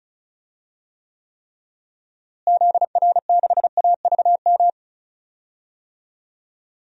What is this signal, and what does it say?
Morse code, heard as a keyed tone, about 34 wpm, 700 Hz. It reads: ZR6AVM
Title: DXCC-100 Call Signs - 6 Characters - Repeated 3x - 35wpm